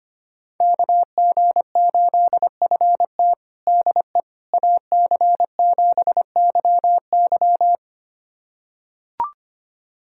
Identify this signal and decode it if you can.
Morse code, heard as a keyed tone, about 25 wpm, 700 Hz. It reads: KG8FT DE AC7YY
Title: USA Callsign DE Callsign - 5 Characters - 25wpm